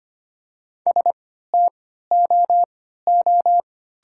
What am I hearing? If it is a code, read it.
Morse code, heard as a keyed tone, about 25 wpm, spaced as for 14 wpm, 690 Hz. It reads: STOO
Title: 3x Character Spacing - Top 500 Words - 25wpm